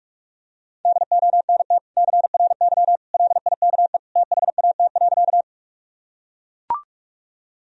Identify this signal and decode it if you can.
Morse code, heard as a keyed tone, about 44 wpm, 700 Hz. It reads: DONT CRY LIKE THAT.
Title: Sentences from Top 1000 Words - Review Part 2 - 45wpm